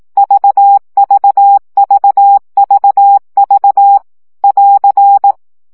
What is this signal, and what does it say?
Morse code, heard as a keyed tone, about 18 wpm, 800 Hz. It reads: VVVVV <AR>
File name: CW gyakorlas v 15 baud